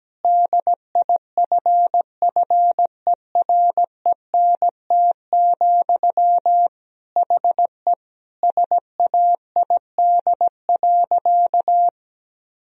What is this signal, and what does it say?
Morse code, heard as a keyed tone, about 17 wpm, 700 Hz. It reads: DIFFERENT, HE SAID.
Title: Sentences from Top 1000 Words - Review Part 3 - 17wpm